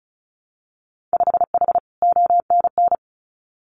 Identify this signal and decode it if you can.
Morse code, heard as a keyed tone, about 36 wpm, 700 Hz. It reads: 5H ONN
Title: Winter Field Day Exchanges Encore 35wpm